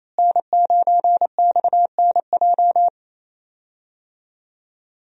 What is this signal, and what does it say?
Morse code, heard as a keyed tone, about 28 wpm, 700 Hz. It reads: N9XNJ
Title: USA Call Signs - 5 Characters - Repeated 3x - 28wpm